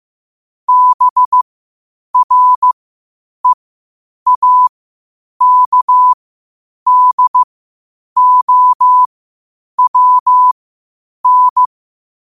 Morse code, heard as a keyed tone, about 15 wpm, spaced as for 8 wpm, 1000 Hz. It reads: BREAKDOWN
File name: qotdfiles / 8wpm